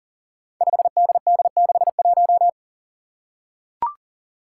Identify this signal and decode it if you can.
Morse code, heard as a keyed tone, about 38 wpm, 700 Hz. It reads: 5DD61
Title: Five Challenging Letter-Number Combinations 40wpm